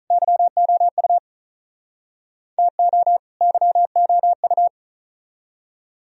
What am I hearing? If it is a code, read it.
Morse code, heard as a keyed tone, about 47 wpm, 700 Hz. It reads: YOU TO YOU